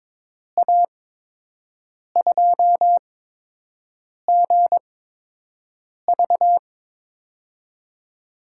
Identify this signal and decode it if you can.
Morse code, heard as a keyed tone, about 22 wpm, spaced as for 6 wpm, 700 Hz. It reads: A2GV
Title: USA Call Signs - 5 Characters - 8x Character Spacing - 22wpm